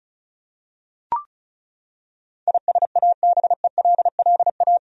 Morse code, heard as a keyed tone, about 34 wpm, 700 Hz. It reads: ISABELLA
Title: USA Common Names - 35wpm